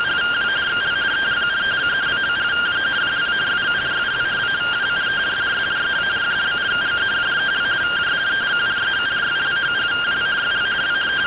Two-tone teleprinter signal, about 170 Hz shift.
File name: findings / CT 1 Scout 0001.gen-35 rtty rx